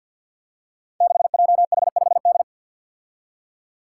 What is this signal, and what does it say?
Morse code, heard as a keyed tone, about 52 wpm, 700 Hz. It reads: 6JH5D
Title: Five Challenging Letter-Number Combinations 50wpm